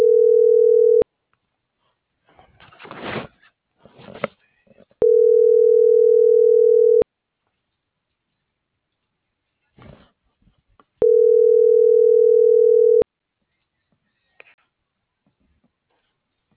Ringback tone (North America), 440 + 480 Hz.